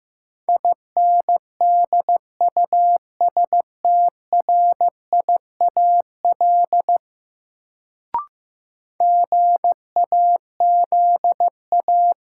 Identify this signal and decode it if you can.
Morse code, heard as a keyed tone, about 15 wpm, 700 Hz. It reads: INDUSTRIAL GAZA
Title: Sets of 5 Words - New Word Drill - 15wpm